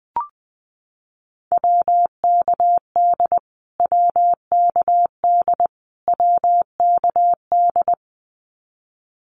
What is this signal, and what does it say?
Morse code, heard as a keyed tone, about 20 wpm, 700 Hz. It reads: WKD WKD WKD
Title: Abbreviations Repeated 3x - 20wpm